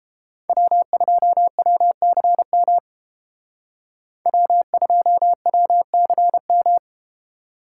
Morse code, heard as a keyed tone, about 30 wpm, 700 Hz. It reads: W2WCM W2WCM